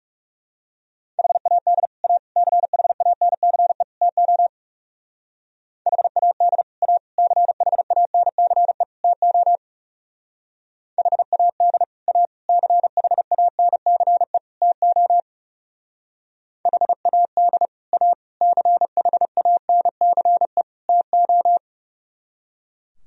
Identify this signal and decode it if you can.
Morse code, heard as a keyed tone, about 38 wpm, 700 Hz. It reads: HAD A CHANCE TO HAD A CHANCE TO HAD A CHANCE TO HAD A CHANCE TO